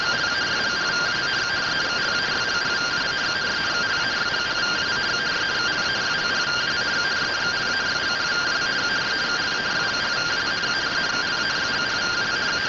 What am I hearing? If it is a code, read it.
Two-tone teleprinter signal, about 170 Hz shift.
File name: findings / SAGITAL nb203 fr61 3mm.gen-34 rtty rx